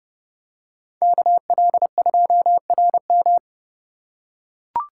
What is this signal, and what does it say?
Morse code, heard as a keyed tone, about 29 wpm, 700 Hz. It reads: KL2RM